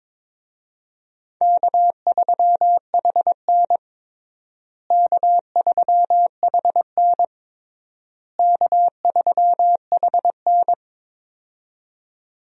Morse code, heard as a keyed tone, about 22 wpm, 700 Hz. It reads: K3HN K3HN K3HN
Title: USA Call Signs - 4 Characters - Repeated 3x - 22wpm